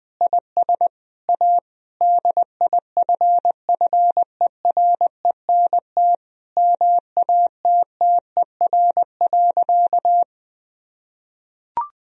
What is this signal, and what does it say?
Morse code, heard as a keyed tone, about 20 wpm, 700 Hz. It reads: IS A DIFFERENT MATTER.
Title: Sentences from Top 1000 Words - Review Part 3 - 20wpm